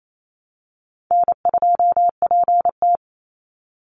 Morse code, heard as a keyed tone, about 28 wpm, 700 Hz. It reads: N2PT
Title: USA Call Signs 28wpm